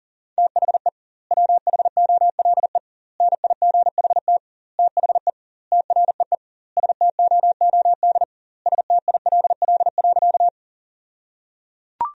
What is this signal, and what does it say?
Morse code, heard as a keyed tone, about 37 wpm, 700 Hz. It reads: THE WHOLE NIGHT THE TREE STOOD STILL.
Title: 2x Word-Spacing - Sentences from Top 500 Words - 40wpm